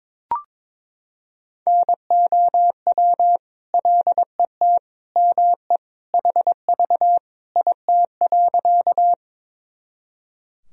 Morse code, heard as a keyed tone, about 22 wpm, 700 Hz. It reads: NOW LET ME HV IT.
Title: Sentences from Top 300 Words and Abbreviations 22wpm